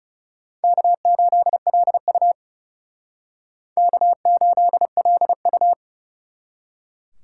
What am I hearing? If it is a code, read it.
Morse code, heard as a keyed tone, about 39 wpm, 700 Hz. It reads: K8LU K8LU